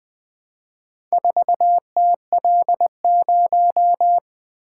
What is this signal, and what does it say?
Morse code, heard as a keyed tone, about 20 wpm, 700 Hz. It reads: 4TL0
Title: Four Letter-Number Combinations - 20wpm